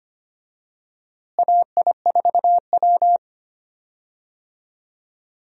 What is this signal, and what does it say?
Morse code, heard as a keyed tone, about 25 wpm, 700 Hz. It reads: AI4W
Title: USA Call Signs - 4 Characters - Repeated 2x - 25wpm